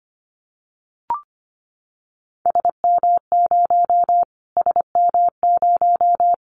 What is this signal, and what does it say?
Morse code, heard as a keyed tone, about 25 wpm, 700 Hz. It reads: SM0 SM0
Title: Callsign Prefixes for Top 25 DXCC - Repeated 3x - 25wpm